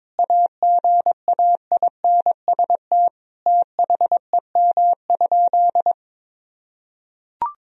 Morse code, heard as a keyed tone, about 22 wpm, 700 Hz. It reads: AGAINST THEM?
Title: Sentences from Top 1000 Words - Review Part 2 - 22wpm